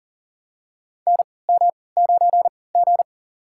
Morse code, heard as a keyed tone, about 40 wpm, spaced as for 23 wpm, 700 Hz. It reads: NM9G
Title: USA Call Signs - 4 Characters - 3x Character Spacing - 40wpm